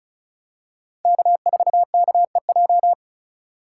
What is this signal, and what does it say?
Morse code, heard as a keyed tone, about 34 wpm, 700 Hz. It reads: K4KEJ
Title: USA Call Signs - 5 Characters - 2x Character Spacing - 35wpm